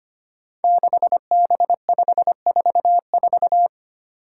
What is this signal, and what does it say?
Morse code, heard as a keyed tone, about 25 wpm, 700 Hz. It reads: 6B544
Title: Five Challenging Letter-Number Combinations 25wpm